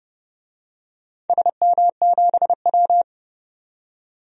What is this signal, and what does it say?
Morse code, heard as a keyed tone, about 31 wpm, 700 Hz. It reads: SM7W
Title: DXCC-100 Call Signs - 4 Characters - 30wpm